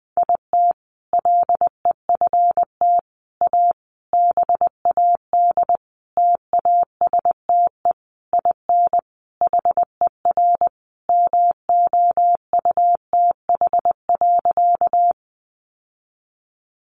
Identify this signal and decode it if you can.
Morse code, heard as a keyed tone, about 20 wpm, 700 Hz. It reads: IT LEFT A BAD TASTE IN HER MOUTH.
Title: Sentences from Top 1500 Words 20wpm